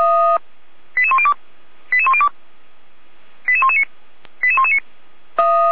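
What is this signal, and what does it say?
Five-tone selective call (ZVEI), heard as quick runs of five tones, tones of about 70 ms. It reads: 88102 88102 88119 88119
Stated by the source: ZVEI five-tone selective call 88105 received on SDR, from the feuerwehr-orlamuende.org fire brigade audio archive.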